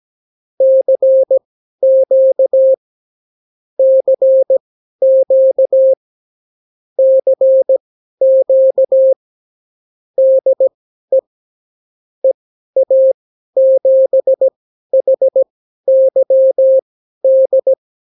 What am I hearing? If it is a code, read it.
Morse code, heard as a keyed tone, about 17 wpm, spaced as for 12 wpm, 530 Hz. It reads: CQ CQ CQ DE EA7HYD
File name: CW QSO libroCW4 18-8ppm